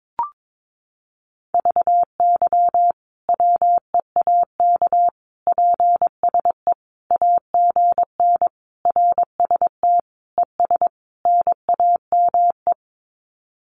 Morse code, heard as a keyed tone, about 22 wpm, 700 Hz. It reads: VY WEAK PSE AGN RST ES NAME
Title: Common QSO Phrases 22wpm